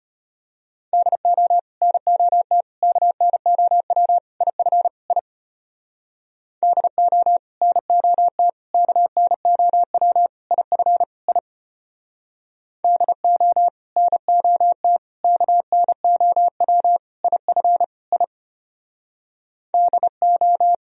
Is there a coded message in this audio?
Yes: Morse code from 0.9 to 20.7 s, heard as a keyed tone, about 32 wpm, 700 Hz. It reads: DO NOT KNOW IF I DO NOT KNOW IF I DO NOT KNOW IF I DO